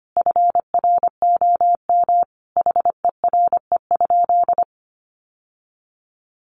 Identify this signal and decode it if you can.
Morse code, heard as a keyed tone, about 25 wpm, 690 Hz. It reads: FROM HERE?
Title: Sentences from Top 1100 Words 25wpm